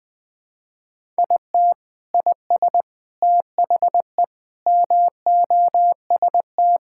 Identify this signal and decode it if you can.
Morse code, heard as a keyed tone, about 20 wpm, 700 Hz. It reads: IT IS THE MOST
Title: Sentences with 8-Word Phrases - Spoken Beginning - 20wpm